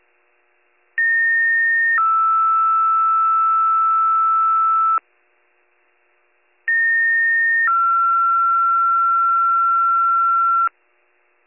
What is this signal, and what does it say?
Two-tone page, heard as a one-second tone, then a three-second tone, about 1850 then 1320 Hz, about 1850 then 1360 Hz. It is heard from 1.0 to 10.7 s.